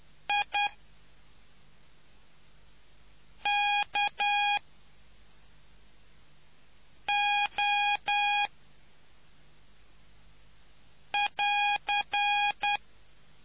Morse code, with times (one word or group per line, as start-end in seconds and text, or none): 0.3-0.7 I
3.5-4.6 K
7.1-8.5 O
11.1-12.8 <AR>